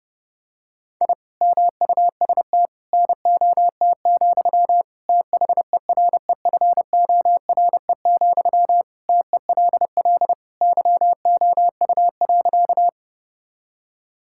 Morse code, heard as a keyed tone, about 30 wpm, 700 Hz. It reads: I MUST NOT, THEREFORE, TELL YOU.
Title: Sentences from Top 1000 Words - Review Part 5 - 30wpm